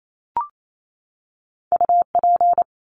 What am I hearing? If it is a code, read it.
Morse code, heard as a keyed tone, about 28 wpm, 700 Hz. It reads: UP
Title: Top Two Letter Combinations in Words 28wpm